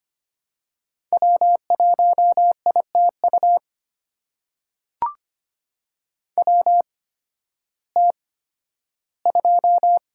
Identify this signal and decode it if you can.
Morse code, heard as a keyed tone, about 25 wpm, spaced as for 13 wpm, 700 Hz. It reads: W1ITU WT2
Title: USA Call Signs - 5 Characters - 8x Character Spacing - 25wpm